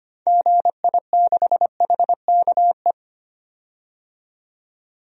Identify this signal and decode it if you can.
Morse code, heard as a keyed tone, about 25 wpm, 700 Hz. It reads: GI6HKE
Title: DXCC-100 Call Signs - 6 Characters - 25wpm